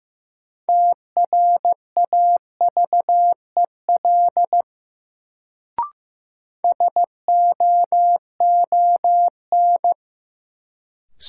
Morse code, heard as a keyed tone, about 15 wpm, 700 Hz. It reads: TRAVEL SOON
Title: Sentences from Top 400 Words - New Word Drill - 15wpm